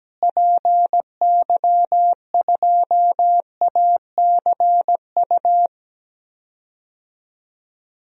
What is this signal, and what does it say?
Morse code, heard as a keyed tone, about 17 wpm, 700 Hz. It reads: PY2ACU
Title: DXCC-100 Call Signs - 6 Characters - Repeated 3x - 17wpm